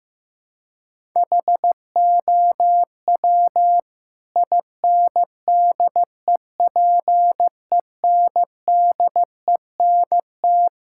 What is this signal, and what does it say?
Morse code, heard as a keyed tone, about 15 wpm, 700 Hz. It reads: HOW INDEPENDENT
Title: Sentences from Top 1000 Words - Review Part 6 - 15wpm